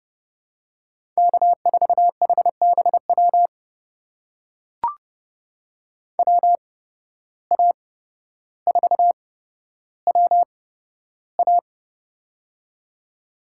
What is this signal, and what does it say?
Morse code, heard as a keyed tone, about 31 wpm, spaced as for 13 wpm, 700 Hz. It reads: K4HBW WA4WA
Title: USA Call Signs - 5 Characters - 8x Character Spacing - 30wpm